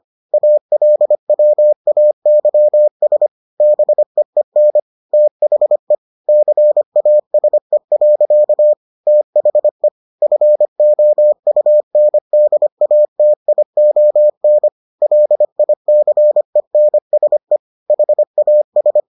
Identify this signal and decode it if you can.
Morse code, heard as a keyed tone, about 25 wpm, 600 Hz. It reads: ALWAYS BEEN THE CASE. THE FOUNDATION LICENSE HAS